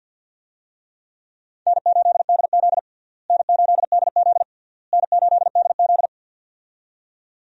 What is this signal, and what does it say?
Morse code, heard as a keyed tone, about 52 wpm, 700 Hz. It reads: N8DZ N8DZ N8DZ
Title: USA Call Signs - 4 Characters - Repeated 3x - 50wpm